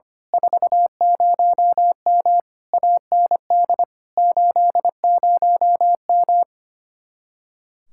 Morse code, heard as a keyed tone, about 27 wpm, 700 Hz. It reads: 40M AND 80M
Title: Speed-Racing Common QSO Phrases - 25wpm